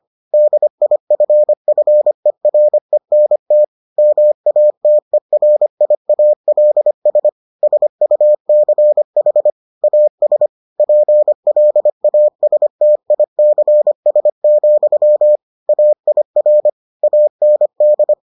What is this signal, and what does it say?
Morse code, heard as a keyed tone, about 25 wpm, 600 Hz. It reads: DIFFERENT MATERIALS SUCH AS PLASTICS, AIR AND